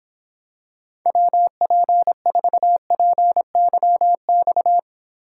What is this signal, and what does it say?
Morse code, heard as a keyed tone, about 26 wpm, 700 Hz. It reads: WP4PYX